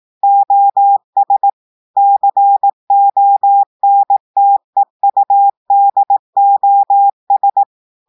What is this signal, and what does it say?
Morse code, heard as a keyed tone, about 18 wpm, 800 Hz. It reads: OS CONTEUDOS
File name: audios / portugais / portugais texte en clair 2 vitesse 18 mots minutes